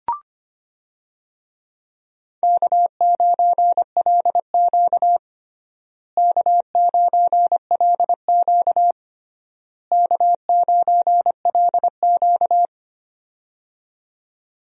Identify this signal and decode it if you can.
Morse code, heard as a keyed tone, about 25 wpm, 700 Hz. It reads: K9LQ K9LQ K9LQ
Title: USA Call Signs - 4 Characters - Repeated 3x - 25wpm